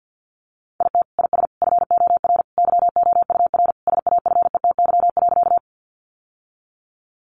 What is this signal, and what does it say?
Morse code, heard as a keyed tone, about 50 wpm, 700 Hz. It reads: IT IS FOR YOUR SAFETY.
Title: Sentences from Top 700 Words 50wpm